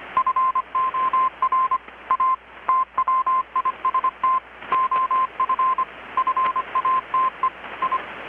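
Morse code, heard as a keyed tone, about 25 wpm, 1040 Hz, with radio noise. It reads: FOR A TWIST OF FATE I
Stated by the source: Morse from KPH's Night of Nights XX maritime radiotelegraphy event, received in Chiapas, Mexico about 3,800 km away on a Kenwood R-600 with a grounded 10 m long-wire antenna.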